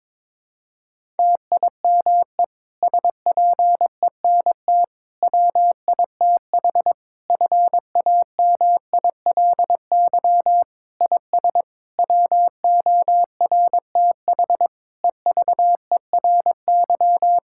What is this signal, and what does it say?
Morse code, heard as a keyed tone, about 21 wpm, 700 Hz. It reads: TIME SPENT WITH FAMILY IS WORTH EVERY
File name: Difficult Sentences with Hints - 22wpm